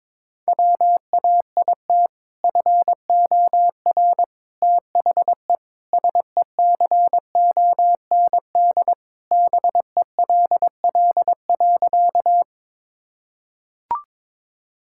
Morse code, heard as a keyed tone, about 21 wpm, 700 Hz. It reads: WAIT FOR THE SECOND BELL.
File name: practice / 22wpm / Sentences with 2-Word Phrases - 22wpm